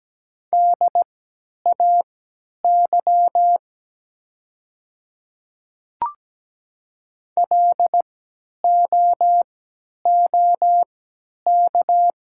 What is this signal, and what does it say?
Morse code, heard as a keyed tone, about 17 wpm, spaced as for 10 wpm, 700 Hz. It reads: DAY LOOK
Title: 3x Character Spacing - Top 100 Words - 17wpm